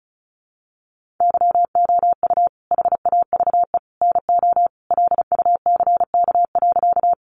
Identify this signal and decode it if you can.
Morse code, heard as a keyed tone, about 34 wpm, 700 Hz. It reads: YOU HAVE NO LUCK.
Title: Sentences from Top 1500 Words 35wpm